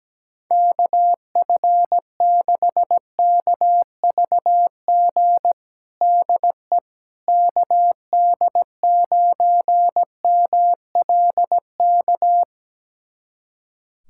Morse code, heard as a keyed tone, about 17 wpm, 700 Hz. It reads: KF6KVG DE KD9MLK
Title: USA Callsign DE Callsign 17wpm